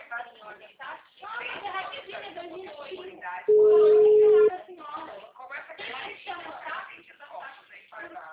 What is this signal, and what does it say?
Ringback tone (Europe, 425 Hz), with speech.
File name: out- -2025- - - .49993